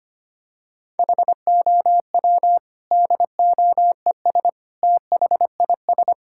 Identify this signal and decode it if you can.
Morse code, heard as a keyed tone, about 25 wpm, 700 Hz. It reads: HOW DOES THIS
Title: Sentences from Top 1000 Words - Review Part 5 - 25wpm